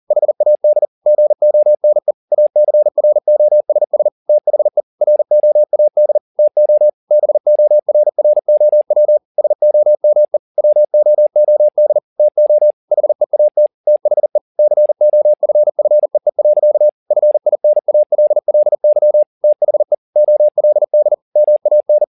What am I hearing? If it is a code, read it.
Morse code, heard as a keyed tone, about 40 wpm, 600 Hz. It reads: HAD GONE ACROSS THE ROAD TO BORROW SOME WOOD TO HEAT THE COFFEE. FINALLY THE OLD MAN
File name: the old man and the sea 2 40wpm